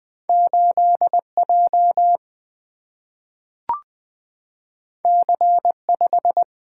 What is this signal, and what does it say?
Morse code, heard as a keyed tone, about 20 wpm, 700 Hz. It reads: 8J C5
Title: Single-Letter Single-Number 20wpm